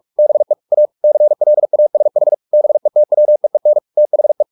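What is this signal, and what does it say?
Morse code, heard as a keyed tone, about 44 wpm, 600 Hz. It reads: BE A CLASH BETWEEN THE
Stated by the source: Morse practice at 45 wpm, the novel Never by Ken Follett in English, from CW med Gnister.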